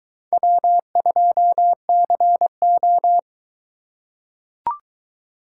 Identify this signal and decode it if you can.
Morse code, heard as a keyed tone, about 22 wpm, 700 Hz. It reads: W2CO